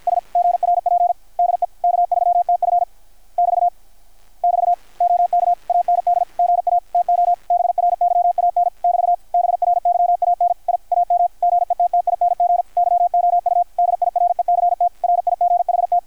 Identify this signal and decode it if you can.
Morse code, heard as a keyed tone, about 53 wpm, 700 Hz, with radio noise. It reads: S GRP DE K3TF <BT> <BT> OK AND GA TO BRYAN <BT> BRYAN I AM GETTING YOU DIRECT RIGHT
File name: archived / CRSnet